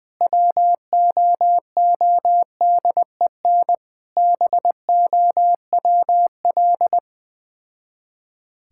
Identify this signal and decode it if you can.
Morse code, heard as a keyed tone, about 20 wpm, 690 Hz. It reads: WOODEN BOWL
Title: Difficult Words with Hints - 20wpm